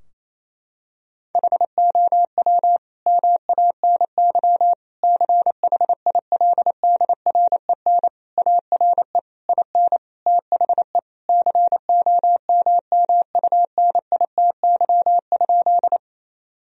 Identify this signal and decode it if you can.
Morse code, heard as a keyed tone, about 28 wpm, 700 Hz. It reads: HOW MANY CHILDREN ARE IN THE COMMUNITY?
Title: Sentences from Top 1000 Words - Review Part 4 - 28wpm